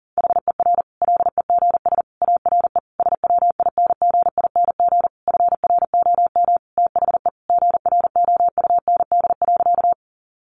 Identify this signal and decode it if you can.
Morse code, heard as a keyed tone, about 40 wpm, 700 Hz. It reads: HER LEGS ARE SWINGING FROM THE GROUND.
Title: Easy Phrases 40wpm